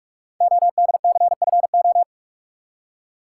Morse code, heard as a keyed tone, about 44 wpm, 700 Hz. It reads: ODCFO
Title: Code Groups - 5 Letters - 45wpm